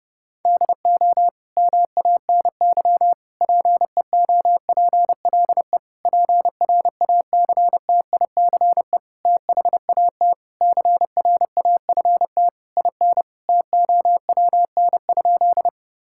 Morse code, heard as a keyed tone, about 30 wpm, 700 Hz. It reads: DO MANY PEOPLE PRACTICE THAT CRAFT IN TOWN?
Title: Sentences from Top 1000 Words - Review Part 6 - 30wpm